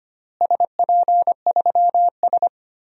Morse code, heard as a keyed tone, about 25 wpm, 700 Hz. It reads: SP3S
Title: DXCC-100 Call Signs - 4 Characters - Repeated 2x - 25wpm